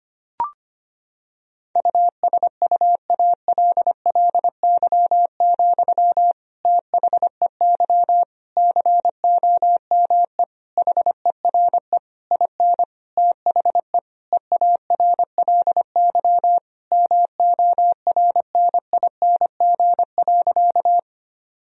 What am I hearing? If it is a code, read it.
Morse code, heard as a keyed tone, about 25 wpm, 700 Hz. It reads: USUALLY, THEY COME HERE IN THE EARLY MORNING.
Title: Sentences from Top 1000 Words - Review Part 3 - 25wpm